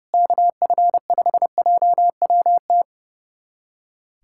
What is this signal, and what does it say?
Morse code, heard as a keyed tone, about 30 wpm, 700 Hz. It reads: KF5JWT
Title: USA Callsign DE Callsign - 6 Characters - 30wpm